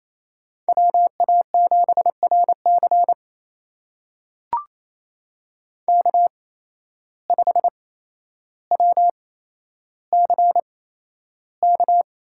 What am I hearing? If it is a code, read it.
Morse code, heard as a keyed tone, about 28 wpm, spaced as for 12 wpm, 700 Hz. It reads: WA7RC K5WCK
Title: USA Call Signs - 5 Characters - 8x Character Spacing - 28wpm